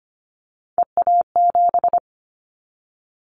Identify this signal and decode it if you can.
Morse code, heard as a keyed tone, about 25 wpm, 700 Hz. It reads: EA7
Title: Callsign Prefixes for Top 25 DXCC - 25wpm